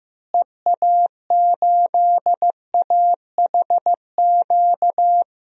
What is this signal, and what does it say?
Morse code, heard as a keyed tone, about 15 wpm, 700 Hz. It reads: EA8AHQ
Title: DXCC-100 Call Signs - 6 Characters - Repeated 3x - 15wpm